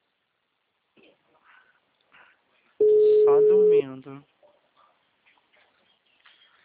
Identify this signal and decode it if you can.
Ringback tone (Europe, 425 Hz), with speech.
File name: out- -2024- - - .34646